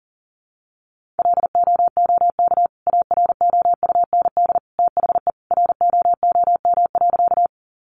Morse code, heard as a keyed tone, about 37 wpm, 700 Hz. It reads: LOOK AROUND THE ROOM.
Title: Sentences from Top 600 Words 40wpm